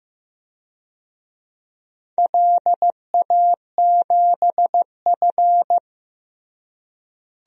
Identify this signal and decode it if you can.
Morse code, heard as a keyed tone, about 16 wpm, 700 Hz. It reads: LA7F
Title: DXCC-100 Call Signs - 4 Characters - Repeated 3x - 15wpm